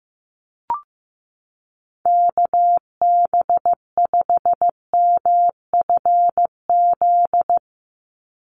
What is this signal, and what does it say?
Morse code, heard as a keyed tone, about 15 wpm, 700 Hz. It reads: KB5MFZ
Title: USA Call Signs 15wpm